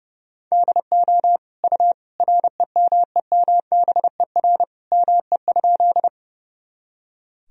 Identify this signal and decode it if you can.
Morse code, heard as a keyed tone, about 32 wpm, 700 Hz. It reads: DO U REMEMBER ME?
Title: Sentences from Top 400 Words and Abbreviations 30wpm